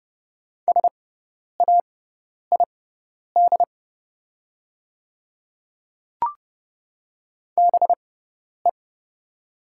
Morse code, heard as a keyed tone, about 31 wpm, spaced as for 10 wpm, 700 Hz. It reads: SAID BE
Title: 6x Character Spacing - Top 100 Words - 30wpm